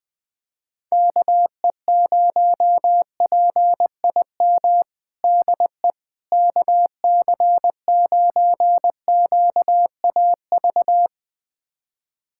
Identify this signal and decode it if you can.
Morse code, heard as a keyed tone, about 20 wpm, 700 Hz. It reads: KE0PIM DE KC9QAV
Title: USA Callsign DE Callsign - 6 Characters - 20wpm